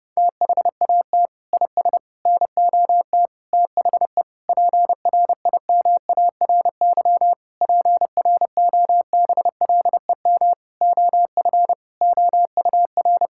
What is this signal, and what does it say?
Morse code, heard as a keyed tone, about 30 wpm, 700 Hz. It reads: THAT IS NOT THE PRIMARY PROBLEM OF OUR
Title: Sentences from Top 1000 Words - Review Part 5 - 30wpm